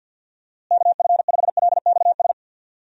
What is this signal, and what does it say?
Morse code, heard as a keyed tone, about 52 wpm, 700 Hz. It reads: KF5LXS
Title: USA Call Signs - 6 Characters - Repeated 2x - 50wpm